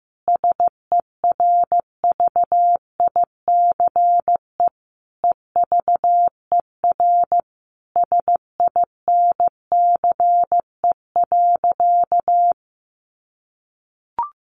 Morse code, heard as a keyed tone, about 15 wpm, 700 Hz. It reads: SERVICE EVER SINCE.
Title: Sentences from Top 700 Words 15wpm